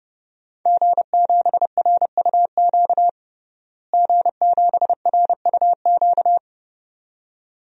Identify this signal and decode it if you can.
Morse code, heard as a keyed tone, about 30 wpm, 700 Hz. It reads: G7RUQ G7RUQ
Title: DXCC-100 Call Signs - 5 Characters - Repeated 2x - 30wpm